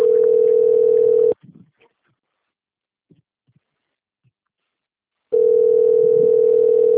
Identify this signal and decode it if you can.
Ringback tone (North America), 440 + 480 Hz.
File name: out- - - - - .61845